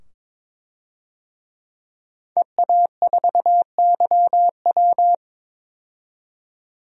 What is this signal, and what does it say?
Morse code, heard as a keyed tone, about 22 wpm, 700 Hz. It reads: EA4YW
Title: DXCC-100 Call Signs - 5 Characters - 22wpm